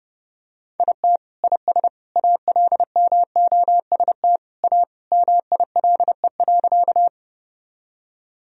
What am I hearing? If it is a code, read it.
Morse code, heard as a keyed tone, about 30 wpm, 700 Hz. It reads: IT IS ALMOST A MILE.
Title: Sentences from Top 1000 Words - Review Part 4 - 30wpm